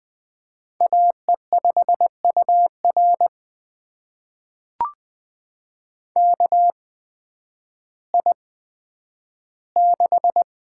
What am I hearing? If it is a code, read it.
Morse code, heard as a keyed tone, about 20 wpm, spaced as for 11 wpm, 700 Hz. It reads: AE5UR KI6
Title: USA Call Signs - 5 Characters - 8x Character Spacing - 20wpm